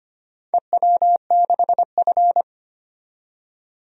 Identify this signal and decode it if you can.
Morse code, heard as a keyed tone, about 25 wpm, 700 Hz. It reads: EW6F
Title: DXCC-100 Call Signs - 4 Characters - 25wpm